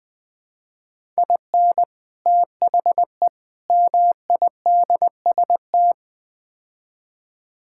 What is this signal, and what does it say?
Morse code, heard as a keyed tone, about 20 wpm, 700 Hz. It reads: IN THE MIDST
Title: Build A Phrase - 4-Word Phrases - 20wpm